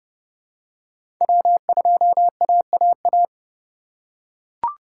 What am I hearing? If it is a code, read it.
Morse code, heard as a keyed tone, about 31 wpm, 700 Hz. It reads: W2AAA
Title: USA Call Signs - 5 Characters - 4x Character Spacing - 30wpm